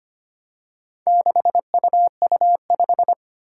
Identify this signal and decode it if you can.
Morse code, heard as a keyed tone, about 25 wpm, 700 Hz. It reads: 6UU5
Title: Four Challenging Letter-Number Combinations 25wpm